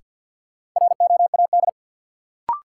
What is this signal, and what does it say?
Morse code, heard as a keyed tone, about 52 wpm, 700 Hz. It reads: ROAD
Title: Sets of 5 Words - New Word Drill - 50wpm